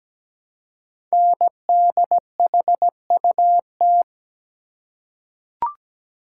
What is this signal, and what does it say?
Morse code, heard as a keyed tone, about 17 wpm, 700 Hz. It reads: NDHUT
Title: Code Groups - 5 Letters - 17wpm